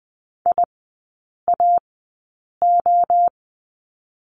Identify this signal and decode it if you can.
Morse code, heard as a keyed tone, about 20 wpm, spaced as for 8 wpm, 700 Hz. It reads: IAO
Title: 041 Lesson 8e - Number 1 - Three Characters 20wpm